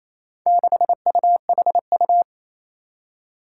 Morse code, heard as a keyed tone, about 28 wpm, 700 Hz. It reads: 6UHU
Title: Four Challenging Letter-Number Combinations 28wpm